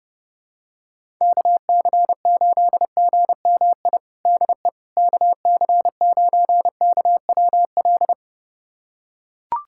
Morse code, heard as a keyed tone, about 32 wpm, 700 Hz. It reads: KC8GMI DE KC9KWL
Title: USA Callsign DE Callsign - 6 Characters - 30wpm